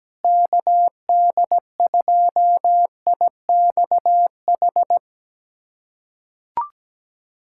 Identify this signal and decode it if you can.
Morse code, heard as a keyed tone, about 17 wpm, 700 Hz. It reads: KD2IXH
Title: USA Callsign DE Callsign - 6 Characters - 17wpm